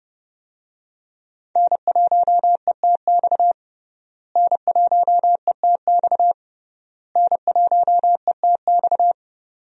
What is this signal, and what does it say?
Morse code, heard as a keyed tone, about 30 wpm, 700 Hz. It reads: N1ETX N1ETX N1ETX
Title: USA Call Signs - 5 Characters - Repeated 3x - 30wpm